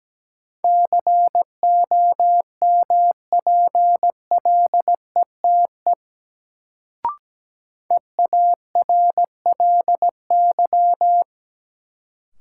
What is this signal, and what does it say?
Morse code, heard as a keyed tone, about 17 wpm, 700 Hz. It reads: COMPLETE EARLY
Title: Sentences from Top 400 Words - New Word Drill - 17wpm